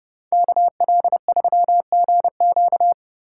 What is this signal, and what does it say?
Morse code, heard as a keyed tone, about 34 wpm, 700 Hz. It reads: KL3GQ